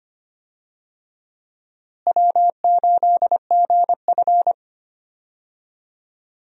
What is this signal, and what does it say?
Morse code, heard as a keyed tone, about 25 wpm, 700 Hz. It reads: W8GF
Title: USA Call Signs - 4 Characters - Repeated 2x - 25wpm